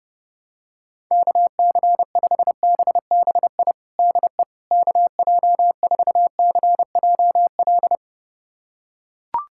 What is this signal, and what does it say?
Morse code, heard as a keyed tone, about 30 wpm, 700 Hz. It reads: KC5BBI DE KJ4CJL
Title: USA Callsign DE Callsign - 6 Characters - 30wpm